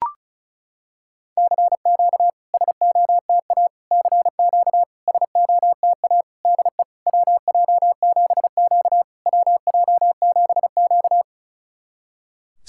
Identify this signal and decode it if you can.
Morse code, heard as a keyed tone, about 34 wpm, 690 Hz. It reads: CQ SOTA CQ SOTA DE WJ7Q WJ7Q
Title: SOTA QSO Practice 35wpm